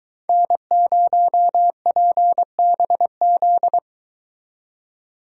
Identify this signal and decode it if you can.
Morse code, heard as a keyed tone, about 23 wpm, 700 Hz. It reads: N0PBZ